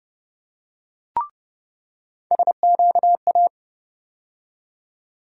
Morse code, heard as a keyed tone, about 30 wpm, 700 Hz. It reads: SQA
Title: USA Call Sign Postfixes - 3 Letters 30wpm